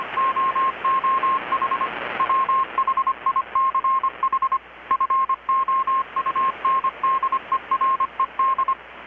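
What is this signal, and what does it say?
Morse code, heard as a keyed tone, about 25 wpm, 1070 Hz, with radio noise. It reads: OOH WHICH FOUNDERED
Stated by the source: Morse from KPH's Night of Nights XX maritime radiotelegraphy event, received in Chiapas, Mexico about 3,800 km away on a Kenwood R-600 with a grounded 10 m long-wire antenna.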